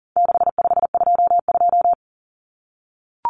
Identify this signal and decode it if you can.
Morse code, heard as a keyed tone, about 40 wpm, 700 Hz. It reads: 6522
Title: Common Cut Numbers - 4-Digit 40wpm